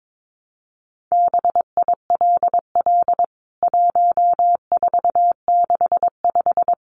Morse code, heard as a keyed tone, about 22 wpm, 700 Hz. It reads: BILL 1465
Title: CWT Exchange 22wpm